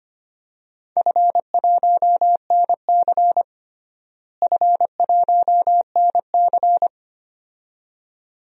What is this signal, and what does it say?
Morse code, heard as a keyed tone, about 25 wpm, 700 Hz. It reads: F1NC F1NC
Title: DXCC-100 Call Signs - 4 Characters - Repeated 2x - 25wpm